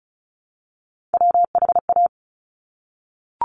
Morse code, heard as a keyed tone, about 35 wpm, 700 Hz. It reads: WHA
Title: Top Three Letter Combinations in Words 35wpm